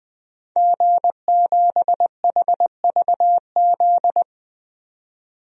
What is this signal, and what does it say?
Morse code, heard as a keyed tone, about 20 wpm, 700 Hz. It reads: G7HVZ
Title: DXCC-100 Call Signs - 5 Characters - 20wpm